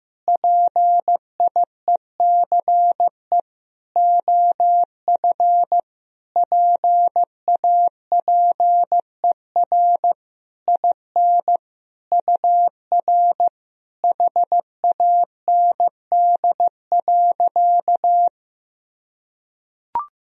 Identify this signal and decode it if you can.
Morse code, heard as a keyed tone, about 15 wpm, 700 Hz. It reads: PIECE OF PAPER IN UR HAND.
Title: Sentences from Top 400 Words and Abbreviations 15wpm